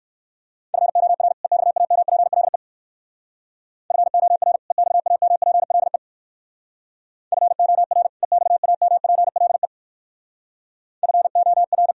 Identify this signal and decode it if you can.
Morse code, heard as a keyed tone, about 59 wpm, 690 Hz. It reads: FOR EXAMPLE FOR EXAMPLE FOR EXAMPLE FOR